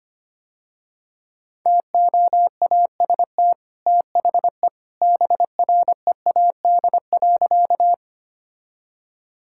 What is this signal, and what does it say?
Morse code, heard as a keyed tone, about 25 wpm, 700 Hz. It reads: TOAST THE BREAD.
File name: Easy Sentences with Hints - 25wpm